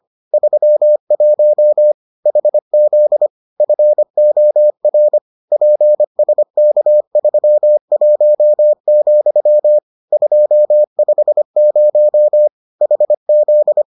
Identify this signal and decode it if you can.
Morse code, heard as a keyed tone, about 25 wpm, 600 Hz. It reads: 31 HZ FOR PSK31, 250 HZ